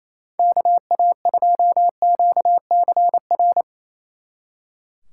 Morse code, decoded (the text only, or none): KA2QCR